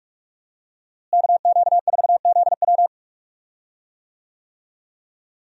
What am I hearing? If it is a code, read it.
Morse code, heard as a keyed tone, about 43 wpm, 700 Hz. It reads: KQ4ZW
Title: USA Call Signs - 5 Characters - Repeated 2x - 45wpm